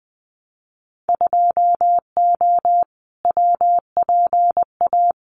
Morse code, heard as a keyed tone, about 20 wpm, 700 Hz. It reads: 2O WPA
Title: Winter Field Day Exchanges 20wpm